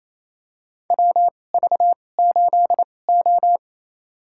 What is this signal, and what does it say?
Morse code, heard as a keyed tone, about 28 wpm, spaced as for 20 wpm, 700 Hz. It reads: WV8O
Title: USA Call Signs - 4 Characters - 2x Character Spacing - 28wpm